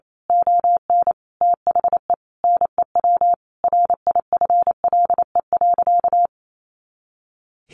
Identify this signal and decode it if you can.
Morse code, heard as a keyed tone, about 28 wpm, 700 Hz. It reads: ON THE NEW RIFLE.
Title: A Sound Of Thunder 28wpm